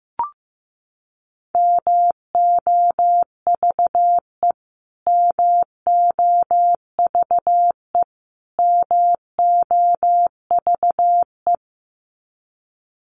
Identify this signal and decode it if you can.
Morse code, heard as a keyed tone, about 15 wpm, 700 Hz. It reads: MOVE MOVE MOVE
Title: Top 300 Words Repeated 15wpm